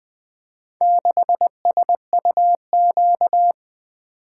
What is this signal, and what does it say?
Morse code, heard as a keyed tone, about 20 wpm, 700 Hz. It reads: 6SUQ
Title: Four Letter-Number Combinations - 20wpm